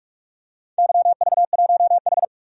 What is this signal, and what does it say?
Morse code, heard as a keyed tone, about 46 wpm, 700 Hz. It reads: YV1H
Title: DXCC-100 Call Signs - 4 Characters - 45wpm